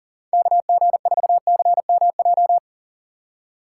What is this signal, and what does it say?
Morse code, heard as a keyed tone, about 39 wpm, 700 Hz. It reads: KG4CMJ
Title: USA Call Signs - 4x Character Spacing - 40wpm